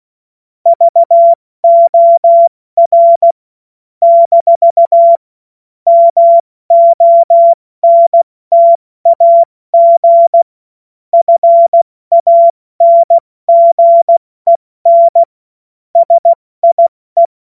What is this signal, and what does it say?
Morse code, heard as a keyed tone, about 16 wpm, 680 Hz. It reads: VOR - MONTAG FANGEN SIE
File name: Morsekurs / Witze / Witz 11. April 70 F80